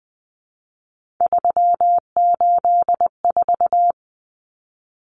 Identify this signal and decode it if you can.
Morse code, heard as a keyed tone, about 20 wpm, 700 Hz. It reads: 384
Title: Common Cut Numbers - 3-Digit 20wpm